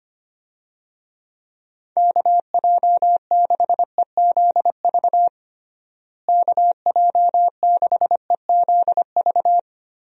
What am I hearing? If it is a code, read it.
Morse code, heard as a keyed tone, about 25 wpm, 700 Hz. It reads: KJ6EZV KJ6EZV
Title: USA Call Signs - 6 Characters - Repeated 2x - 25wpm